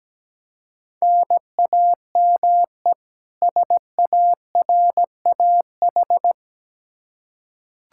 Morse code, heard as a keyed tone, about 17 wpm, 700 Hz. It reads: NAME SARAH
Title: K1USN SST QSO Practice 17wpm